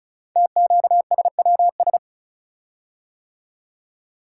Morse code, heard as a keyed tone, about 36 wpm, 690 Hz. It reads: TQSWS
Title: Code Groups - 5 Letters - 35wpm